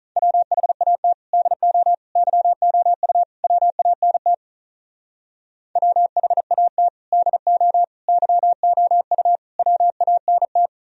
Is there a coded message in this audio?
Yes: Morse code from 0.2 to 10.7 s, heard as a keyed tone, about 44 wpm, 700 Hz. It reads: WHAT DO YOU WANT WHAT DO YOU WANT